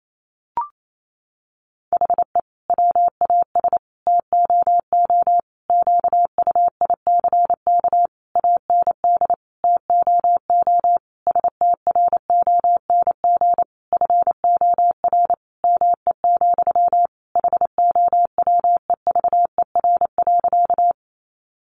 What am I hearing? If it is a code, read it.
Morse code, heard as a keyed tone, about 28 wpm, 700 Hz. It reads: HE WAS TOO QUICK AND TOO STRONG FOR ME, HOWEVER.
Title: Sentences from Top 600 Words 28wpm